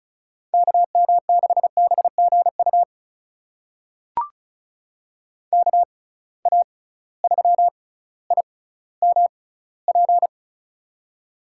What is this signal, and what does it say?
Morse code, heard as a keyed tone, about 36 wpm, spaced as for 18 wpm, 700 Hz. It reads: KM6BGU KA3IMP
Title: USA Call Signs - 6x Character Spacing - 35wpm